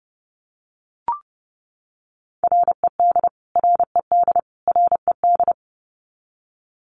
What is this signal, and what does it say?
Morse code, heard as a keyed tone, about 28 wpm, 700 Hz. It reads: RED RED RED
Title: Top 500 Words Repeated 30wpm